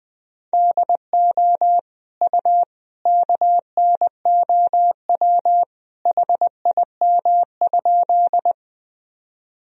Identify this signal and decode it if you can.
Morse code, heard as a keyed tone, about 21 wpm, 700 Hz. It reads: DO U KNOW HIM?
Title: Sentences from Top 100 Words and Abbreviations 20wpm